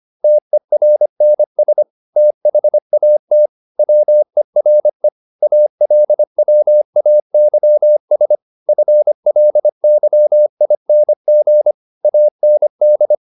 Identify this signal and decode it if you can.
Morse code, heard as a keyed tone, about 25 wpm, 600 Hz. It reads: TERNS THAT WERE ALWAYS FLYING AND
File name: the old man and the sea part 1 25wpm